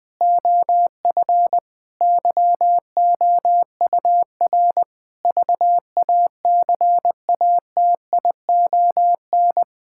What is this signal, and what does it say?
Morse code, heard as a keyed tone, about 20 wpm, 700 Hz. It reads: OF YOUR VACATION
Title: Sentences with 7-Word Phrases - 20wpm